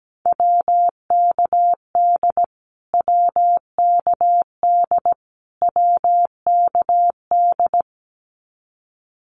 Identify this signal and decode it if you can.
Morse code, heard as a keyed tone, about 17 wpm, 700 Hz. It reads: WKD WKD WKD
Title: Abbreviations Repeated 3x - 17wpm